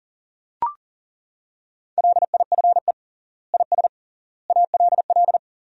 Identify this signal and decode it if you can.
Morse code, heard as a keyed tone, about 40 wpm, 700 Hz. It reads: LIFE IS ALL
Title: 3x Word-Spacing - Sentences from Top 300 Words - 40wpm